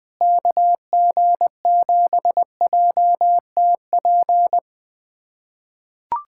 Morse code, heard as a keyed tone, about 21 wpm, 700 Hz. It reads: KG7JTP
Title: USA Callsign DE Callsign - 6 Characters - 20wpm